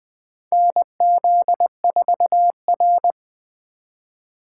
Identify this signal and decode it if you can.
Morse code, heard as a keyed tone, about 20 wpm, 700 Hz. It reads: NZ4R
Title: 2023 SOTA Activator Call Signs - 20wpm